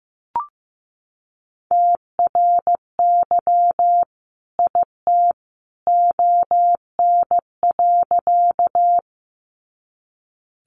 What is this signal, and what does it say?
Morse code, heard as a keyed tone, about 15 wpm, 700 Hz. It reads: TRY IT ON.
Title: Sentences from Top 200 Words 15wpm